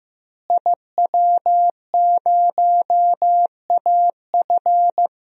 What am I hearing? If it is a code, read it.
Morse code, heard as a keyed tone, about 15 wpm, 700 Hz. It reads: IW0AF
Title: DXCC-100 Call Signs - 5 Characters - Repeated 2x - 15wpm